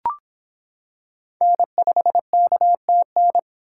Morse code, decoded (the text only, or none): N5KTN